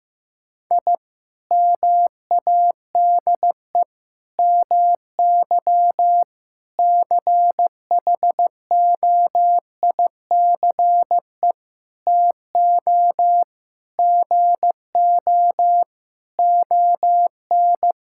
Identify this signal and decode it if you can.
Morse code, heard as a keyed tone, about 15 wpm, 700 Hz. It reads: I MADE MY CHOICE TO GO ON
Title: Sentences from Top 1000 Words - Review Part 6 - 15wpm